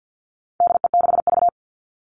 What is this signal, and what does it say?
Morse code, heard as a keyed tone, about 50 wpm, 700 Hz. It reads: DE6V
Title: All Cut Numbers - 4-Digit 50wpm